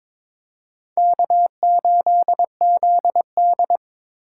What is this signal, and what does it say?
Morse code, heard as a keyed tone, about 23 wpm, 700 Hz. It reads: K8ZD
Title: USA Call Signs - 4 Characters - 3x Character Spacing - 22wpm